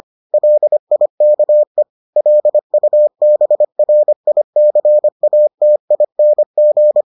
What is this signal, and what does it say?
Morse code, heard as a keyed tone, about 25 wpm, 600 Hz. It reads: LIKE LUBRICATING